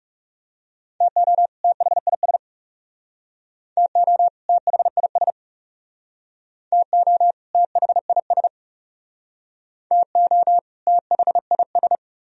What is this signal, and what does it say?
Morse code, heard as a keyed tone, about 37 wpm, 700 Hz. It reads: TO THIS TO THIS TO THIS TO THIS
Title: Speed-Racing - Sets of 2 Words 30wpm